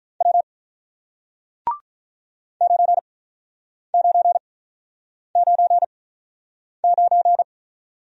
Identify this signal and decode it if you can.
Morse code, heard as a keyed tone, about 47 wpm, spaced as for 9 wpm, 700 Hz. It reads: W 9999